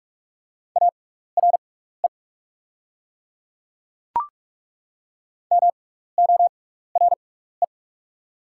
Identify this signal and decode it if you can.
Morse code, heard as a keyed tone, about 43 wpm, spaced as for 15 wpm, 700 Hz. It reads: ARE MORE
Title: 6x Character Spacing - Top 200 Words - 45wpm